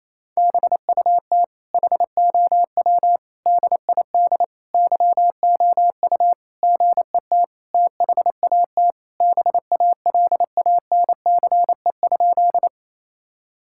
Morse code, heard as a keyed tone, about 28 wpm, 700 Hz. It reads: BUT HOW DID YOU GET THAT BALANCE?
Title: Sentences from Top 1000 Words - Review Part 6 - 28wpm